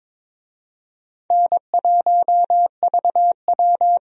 Morse code, heard as a keyed tone, about 22 wpm, 700 Hz. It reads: N1VW
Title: USA Call Signs - 4 Characters - Repeated 2x - 22wpm